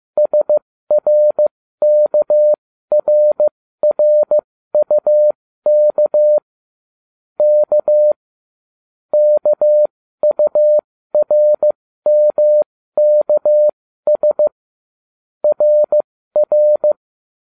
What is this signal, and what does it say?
Morse code, heard as a keyed tone, about 15 wpm, spaced as for 13 wpm, 600 Hz. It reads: SRKRRUK K KURMKS RR